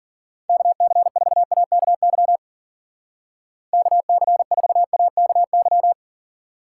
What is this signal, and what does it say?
Morse code, heard as a keyed tone, about 52 wpm, 700 Hz. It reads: KC4AKY KC4AKY